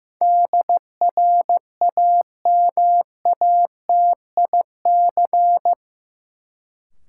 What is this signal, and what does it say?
Morse code, heard as a keyed tone, about 15 wpm, 700 Hz. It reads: DRAMATIC
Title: Sentences from Top 1300 Words - New Word Drill Alt - 15wpm